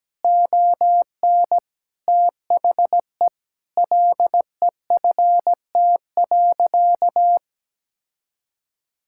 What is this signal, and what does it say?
Morse code, heard as a keyed tone, about 17 wpm, 700 Hz. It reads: ON THE LEFT.
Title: Sentences from Top 1600 Words 17wpm